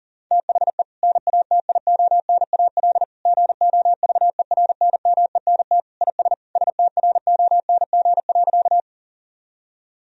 Morse code, heard as a keyed tone, about 40 wpm, 700 Hz. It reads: THE NATIONAL GOVERNMENT IS STRONG.
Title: Sentences from Top 1000 Words - Review Part 4 - 40wpm